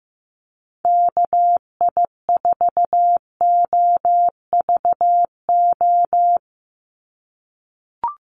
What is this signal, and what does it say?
Morse code, heard as a keyed tone, about 15 wpm, 700 Hz. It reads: KI4OVO
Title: USA Call Signs 15wpm